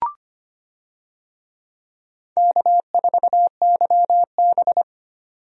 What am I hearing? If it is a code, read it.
Morse code, heard as a keyed tone, about 25 wpm, 700 Hz. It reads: K4YB
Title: USA Call Signs - 4 Characters - 25wpm